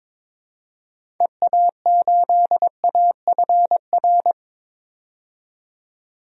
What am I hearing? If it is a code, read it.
Morse code, heard as a keyed tone, about 22 wpm, 700 Hz. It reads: EA8AFR
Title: DXCC-100 Call Signs - 6 Characters - 22wpm